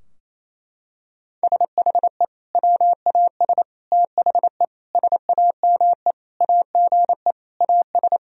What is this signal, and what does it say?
Morse code, heard as a keyed tone, about 28 wpm, 700 Hz. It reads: SHE WAS THE SAME AGE AS
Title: Sentences from Top 1000 Words - Review Part 3 - 28wpm